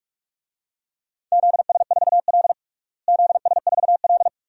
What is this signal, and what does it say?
Morse code, heard as a keyed tone, about 46 wpm, 700 Hz. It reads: ZS4L ZS4L
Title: DXCC-100 Call Signs - 4 Characters - Repeated 2x - 45wpm